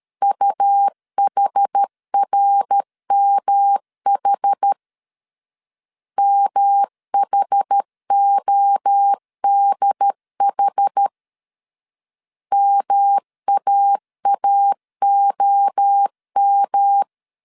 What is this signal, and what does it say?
Morse code, heard as a keyed tone, about 13 wpm, 800 Hz. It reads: UHRMH MHODH MAAOM
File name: CW / DET / 1 3 1500C S C 12wpm 2750